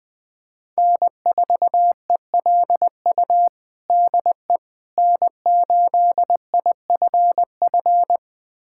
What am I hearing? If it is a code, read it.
Morse code, heard as a keyed tone, about 20 wpm, 700 Hz. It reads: N4ELU DE N8IFF
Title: USA Callsign DE Callsign - 5 Characters - 20wpm